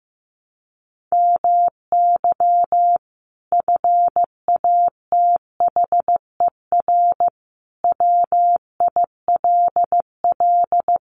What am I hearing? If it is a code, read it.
Morse code, heard as a keyed tone, about 15 wpm, 700 Hz. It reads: MY FATHER WILL
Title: Sentences from Top 1100 Words 15wpm